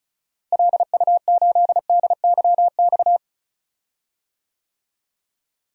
Morse code, heard as a keyed tone, about 36 wpm, 700 Hz. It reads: LU8DYX
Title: DXCC-100 Call Signs - 6 Characters - Repeated 3x - 35wpm